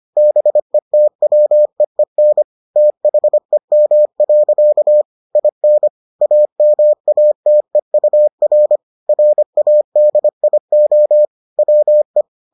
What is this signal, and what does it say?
Morse code, heard as a keyed tone, about 25 wpm, 600 Hz. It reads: BETWEEN THEM. IN AMATEUR RADIO WE